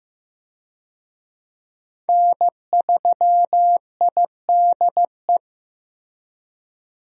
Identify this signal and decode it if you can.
Morse code, heard as a keyed tone, about 16 wpm, 700 Hz. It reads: N3IDE
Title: USA Call Signs - 5 Characters - Repeated 2x - 15wpm